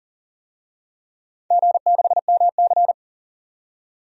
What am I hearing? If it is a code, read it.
Morse code, heard as a keyed tone, about 38 wpm, 700 Hz. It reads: G6MC
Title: DXCC-100 Call Signs - 4 Characters - 40wpm